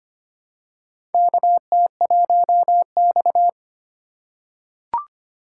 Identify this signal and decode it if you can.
Morse code, heard as a keyed tone, about 25 wpm, 700 Hz. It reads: KT1X
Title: USA Call Signs - 2x Character Spacing - 25wpm